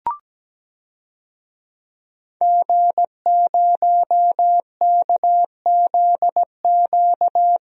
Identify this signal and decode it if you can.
Morse code, heard as a keyed tone, about 17 wpm, 700 Hz. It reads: G0KZQ
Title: DXCC-100 Call Signs - 5 Characters - 17wpm